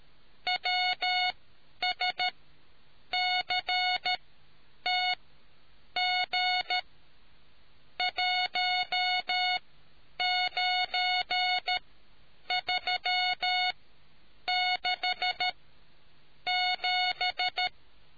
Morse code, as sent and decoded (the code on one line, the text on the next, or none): .-- ... -.-. - --. .---- ----. ...-- -.... --...
WSCTG19367